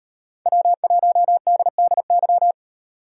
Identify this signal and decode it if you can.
Morse code, heard as a keyed tone, about 33 wpm, 700 Hz. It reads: W1DDY
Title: Speed Racing - USA 1x3 Call Signs 28wpm